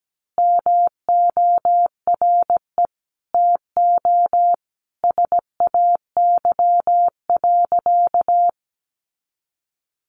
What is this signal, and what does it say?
Morse code, heard as a keyed tone, about 17 wpm, 700 Hz. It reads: MORE TO SAY.
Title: Sentences from Top 500 Words 17wpm